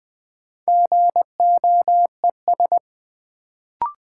Morse code, heard as a keyed tone, about 20 wpm, 700 Hz. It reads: GOES
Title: Sentences from Top 1000 Words - Review Part 5 - New Word Drill - 20wpm